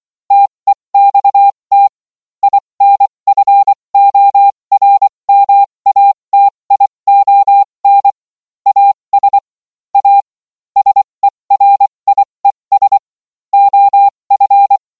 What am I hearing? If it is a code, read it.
Morse code, heard as a keyed tone, about 24 wpm, spaced as for 20 wpm, 800 Hz. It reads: TEXT INFORMATION AS A SERIES OF